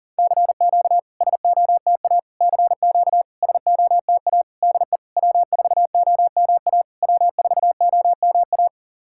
Morse code, heard as a keyed tone, about 39 wpm, 700 Hz. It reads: CQ SOTA CQ SOTA DE W4OMA W4OMA
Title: SOTA QSO Practice 40wpm